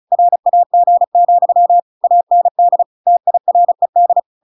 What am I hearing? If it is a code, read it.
Morse code, heard as a keyed tone, about 34 wpm, 700 Hz. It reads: RAG, AND TIRED